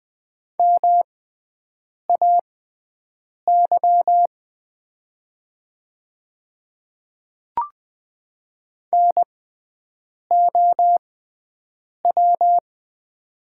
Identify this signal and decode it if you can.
Morse code, heard as a keyed tone, about 20 wpm, spaced as for 7 wpm, 700 Hz. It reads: MAY NOW
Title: 6x Character Spacing - Top 200 Words - 20wpm